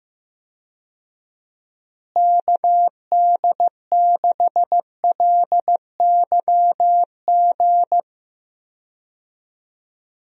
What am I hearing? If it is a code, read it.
Morse code, heard as a keyed tone, about 15 wpm, 700 Hz. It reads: KD6LYG
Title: USA Call Signs - 6 Characters - Repeated 2x - 15wpm